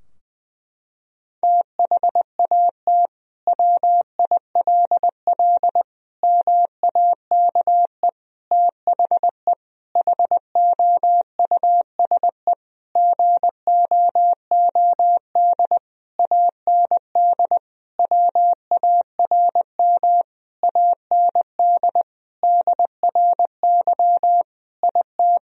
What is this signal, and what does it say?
Morse code, heard as a keyed tone, about 20 wpm, 700 Hz. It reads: THAT WILL MAKE THE HOUSE GOOD AND WARM AND DRY IT
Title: Sentences from Top 1000 Words - Review Part 3 - 20wpm